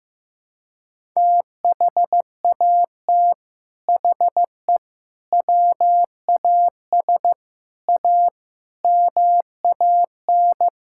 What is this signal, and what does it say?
Morse code, heard as a keyed tone, about 15 wpm, 700 Hz. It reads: THAT HE WAS A MAN
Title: Sentences from Top 1000 Words - Review Part 6 - 15wpm